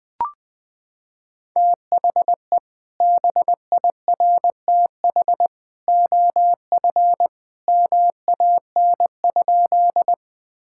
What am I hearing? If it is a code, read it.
Morse code, heard as a keyed tone, about 20 wpm, 700 Hz. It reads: THE BIRTH OF MAN?
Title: Sentences from Top 1000 Words - Review Part 5 - 20wpm